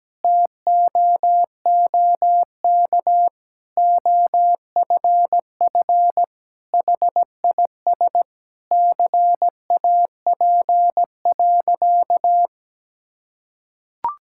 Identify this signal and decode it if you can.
Morse code, heard as a keyed tone, about 17 wpm, 700 Hz. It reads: TOOK OFF HIS CAP.
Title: Sentences from Top 1600 Words 17wpm